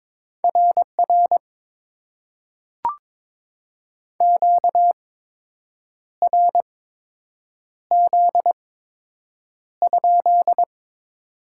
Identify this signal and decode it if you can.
Morse code, heard as a keyed tone, about 22 wpm, spaced as for 7 wpm, 700 Hz. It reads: RR QRZ?
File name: Top 90 QSO Elements - 8x Character Spacing - 22wpm